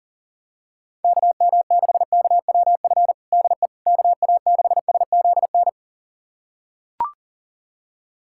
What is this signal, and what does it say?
Morse code, heard as a keyed tone, about 40 wpm, 700 Hz. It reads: KM6KWF DE KA6SZN
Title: USA Callsign DE Callsign - 6 Characters - 40wpm